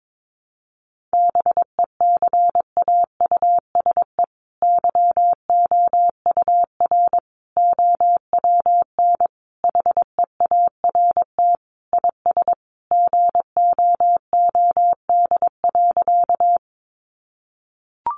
Morse code, heard as a keyed tone, about 22 wpm, 700 Hz. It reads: BECAUSE YOUR OWN HEART IS GOOD.
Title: Sentences from Top 1200 Words 22wpm